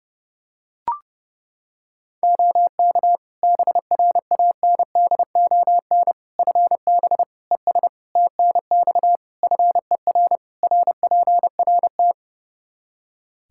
Morse code, heard as a keyed tone, about 30 wpm, 700 Hz. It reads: OK BRANDON FB ES TNX FER RPRT
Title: Practice QSO - Exchange 2 - Info - 30wpm